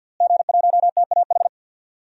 Morse code, heard as a keyed tone, about 46 wpm, 700 Hz. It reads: G1TAH
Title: DXCC-100 Call Signs - 5 Characters - Repeated 3x - 50wpm